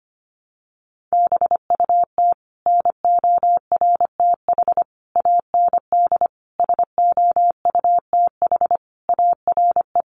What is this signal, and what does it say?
Morse code, heard as a keyed tone, about 25 wpm, 700 Hz. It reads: BUT NORTH AND SOUTH ARE
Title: Sentences from Top 1100 Words 25wpm